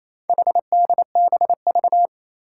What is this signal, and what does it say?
Morse code, heard as a keyed tone, about 28 wpm, 700 Hz. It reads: HDBV
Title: Four Challenging Letter-Number Combinations 28wpm